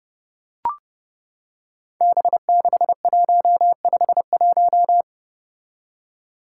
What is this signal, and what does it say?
Morse code, heard as a keyed tone, about 30 wpm, 700 Hz. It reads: B6151
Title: Five Challenging Letter-Number Combinations 30wpm